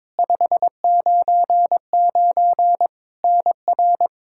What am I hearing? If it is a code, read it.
Morse code, heard as a keyed tone, about 22 wpm, 700 Hz. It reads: 599 NR
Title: Practice QSO - Exchange 1 - Beginning - 22wpm